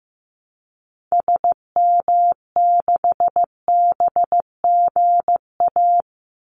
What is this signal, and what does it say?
Morse code, heard as a keyed tone, about 15 wpm, 700 Hz. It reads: SM6BGA
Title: Call Signs Repeated 2x - 15wpm